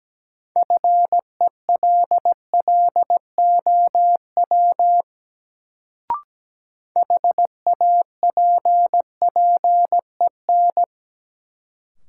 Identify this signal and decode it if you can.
Morse code, heard as a keyed tone, about 17 wpm, 700 Hz. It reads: FELLOW HAPPEN
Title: Sets of 5 Words Encore - New Word Drill - 17wpm